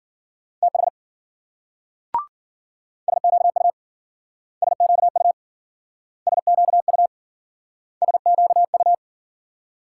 Morse code, heard as a keyed tone, about 53 wpm, 700 Hz. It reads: TH SQU SQU SQU SQU